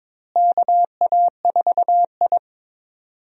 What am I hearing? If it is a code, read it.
Morse code, heard as a keyed tone, about 22 wpm, 700 Hz. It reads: KA4I
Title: USA Call Signs - 4 Characters - Repeated 3x - 22wpm